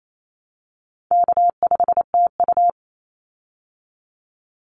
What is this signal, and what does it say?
Morse code, heard as a keyed tone, about 28 wpm, 700 Hz. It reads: K5TU
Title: Call Signs Repeated 2x - 28wpm